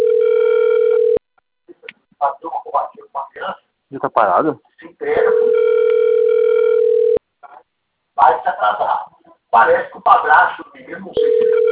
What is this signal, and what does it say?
Ringback tone (North America), 440 + 480 Hz, with speech.
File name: out- - - - - .99515